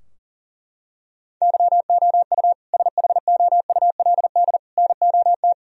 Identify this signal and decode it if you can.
Morse code, heard as a keyed tone, about 40 wpm, 700 Hz. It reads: YOU SHOULD NOT
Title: Sentences from Top 1000 Words - Review Part 3 - 40wpm